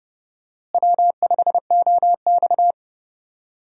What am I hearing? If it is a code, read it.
Morse code, heard as a keyed tone, about 31 wpm, 700 Hz. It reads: W5OX
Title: USA Call Signs - 4 Characters - Repeated 2x - 30wpm